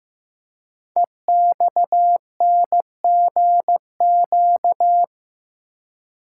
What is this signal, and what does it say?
Morse code, heard as a keyed tone, about 15 wpm, 700 Hz. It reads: EXNGQ
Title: Code Groups - 5 Letters - 15wpm